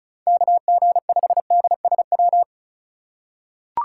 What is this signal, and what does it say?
Morse code, heard as a keyed tone, about 36 wpm, 700 Hz. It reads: KG5DSW
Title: USA Call Signs - 6 Characters - 3x Character Spacing - 35wpm